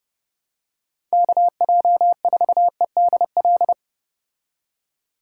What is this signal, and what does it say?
Morse code, heard as a keyed tone, about 29 wpm, 700 Hz. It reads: KJ4EDL